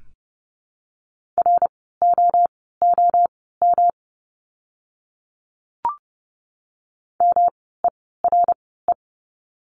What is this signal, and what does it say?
Morse code, heard as a keyed tone, about 30 wpm, spaced as for 17 wpm, 700 Hz. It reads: ROOM MERE
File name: course / 30:10wpm / 108 Lesson 18f - Letter m - Words 30wpm